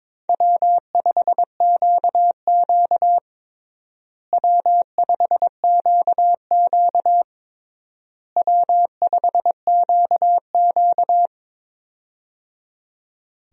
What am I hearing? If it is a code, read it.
Morse code, heard as a keyed tone, about 22 wpm, 700 Hz. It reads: W5QQ W5QQ W5QQ
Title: USA Call Signs - 4 Characters - Repeated 3x - 22wpm